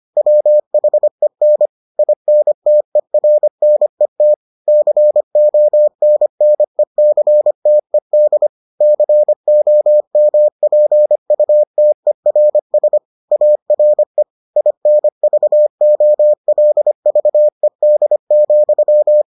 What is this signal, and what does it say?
Morse code, heard as a keyed tone, about 25 wpm, 600 Hz. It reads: WHEN INTERNET CONNECTED COMPUTERS ARE INVOLVED,